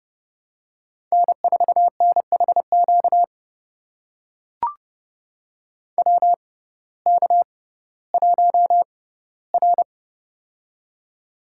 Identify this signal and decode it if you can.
Morse code, heard as a keyed tone, about 32 wpm, spaced as for 17 wpm, 700 Hz. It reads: N4NHQ WK1R
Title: USA Call Signs - 6x Character Spacing - 30wpm